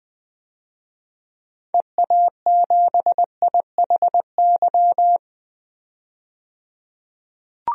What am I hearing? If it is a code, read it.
Morse code, heard as a keyed tone, about 20 wpm, 700 Hz. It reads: EA7IHY
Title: DXCC-100 Call Signs - 6 Characters - 20wpm